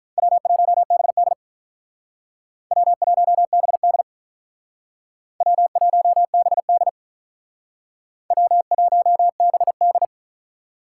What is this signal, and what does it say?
Morse code, heard as a keyed tone, about 45 wpm, 700 Hz. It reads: W1BD W1BD W1BD W1BD